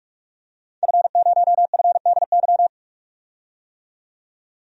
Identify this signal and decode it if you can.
Morse code, heard as a keyed tone, about 46 wpm, 700 Hz. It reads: F0FDY
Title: DXCC-100 Call Signs - 5 Characters - 45wpm